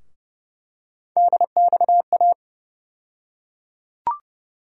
Morse code, heard as a keyed tone, about 30 wpm, 700 Hz. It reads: DXA
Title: USA Call Sign Postfixes - 3 Letters 30wpm